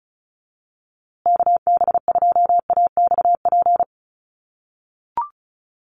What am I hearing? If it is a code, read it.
Morse code, heard as a keyed tone, about 34 wpm, 700 Hz. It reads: KB2AXP
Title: USA Call Signs Repeated 2x 35wpm